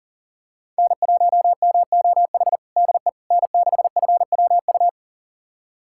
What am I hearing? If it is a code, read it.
Morse code, heard as a keyed tone, about 40 wpm, 700 Hz. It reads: N1MOH DE N6FWU
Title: USA Callsign DE Callsign - 5 Characters - 40wpm